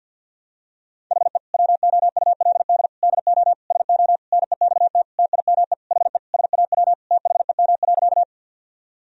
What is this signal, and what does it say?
Morse code, heard as a keyed tone, about 52 wpm, 700 Hz. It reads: HE WOULD DO SO NEXT TIME HE SAW THEM.
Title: Sentences from Top 1000 Words - Review Part 1 - 50wpm